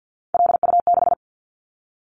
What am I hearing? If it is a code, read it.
Morse code, heard as a keyed tone, about 50 wpm, 700 Hz. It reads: LU6
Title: Callsign Prefixes for Top 50 DXCC - 50wpm